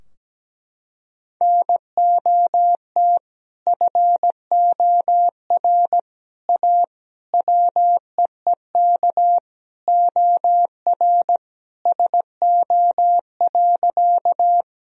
Morse code, heard as a keyed tone, about 17 wpm, 700 Hz. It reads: NOT FOR A WEEK OR SO.
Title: Sentences from Top 1000 Words - Review Part 2 - 17wpm